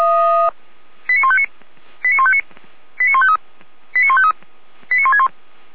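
Five-tone selective call (ZVEI), heard as quick runs of five tones, tones of about 70 ms. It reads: 89159 89159 89153 89153 89161
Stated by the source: ZVEI five-tone selective call received on an SDR.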